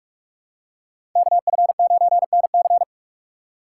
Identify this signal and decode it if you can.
Morse code, heard as a keyed tone, about 47 wpm, 700 Hz. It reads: KF9NC
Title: USA Call Signs - 5 Characters - 6x Character Spacing - 45wpm